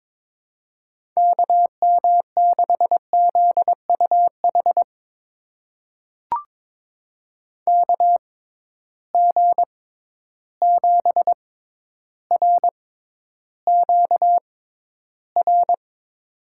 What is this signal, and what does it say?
Morse code, heard as a keyed tone, about 23 wpm, spaced as for 11 wpm, 700 Hz. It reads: KM6ZUH KG7RQR
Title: USA Call Signs - 6 Characters - 6x Character Spacing - 22wpm